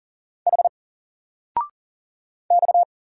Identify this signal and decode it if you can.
Morse code, heard as a keyed tone, about 38 wpm, 700 Hz. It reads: H X
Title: Single Letter-Number 40wpm